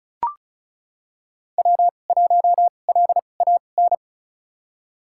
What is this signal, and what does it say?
Morse code, heard as a keyed tone, about 36 wpm, spaced as for 25 wpm, 700 Hz. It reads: W1LAN
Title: USA Call Signs - 5 Characters - 2x Character Spacing - 35wpm